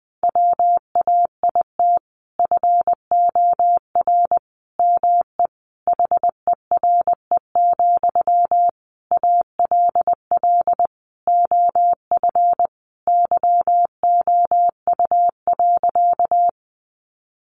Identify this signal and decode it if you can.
Morse code, heard as a keyed tone, about 19 wpm, 700 Hz. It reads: WAIT FOR ME HERE, ALL OF YOU.
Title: Sentences from Top 500 Words 20wpm